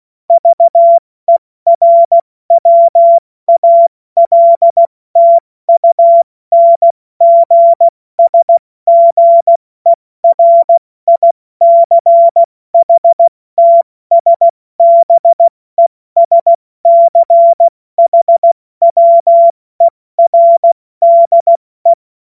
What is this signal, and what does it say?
Morse code, heard as a keyed tone, about 16 wpm, 680 Hz. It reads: VERWALTUNGSGERICHTSBESCHWERDE
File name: Morsekurs / Witze / Witz 30. August 2022 70 F80